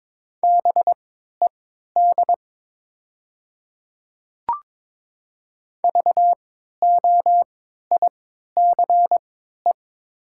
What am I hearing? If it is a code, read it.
Morse code, heard as a keyed tone, about 22 wpm, spaced as for 12 wpm, 700 Hz. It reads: BED VOICE
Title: 3x Character Spacing - Top 500 Words - 22wpm